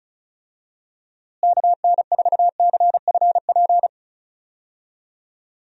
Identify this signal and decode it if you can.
Morse code, heard as a keyed tone, about 36 wpm, 700 Hz. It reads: KN4CFP
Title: USA Call Signs - 6 Characters - Repeated 3x - 35wpm